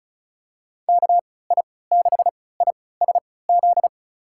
Morse code, heard as a keyed tone, about 34 wpm, spaced as for 20 wpm, 700 Hz. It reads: KI6ISZ
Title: USA Call Signs - 3x Character Spacing - 35wpm